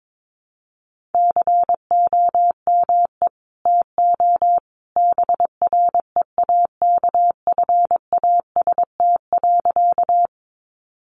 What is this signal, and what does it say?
Morse code, heard as a keyed tone, about 22 wpm, 700 Hz. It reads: COME TO BREAKFAST.
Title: Sentences from Top 1500 Words 22wpm